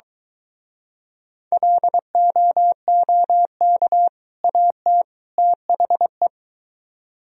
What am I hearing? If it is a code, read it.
Morse code, heard as a keyed tone, about 23 wpm, 700 Hz. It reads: LOOK AT THE